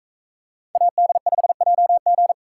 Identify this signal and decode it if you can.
Morse code, heard as a keyed tone, about 42 wpm, 700 Hz. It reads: AD5JG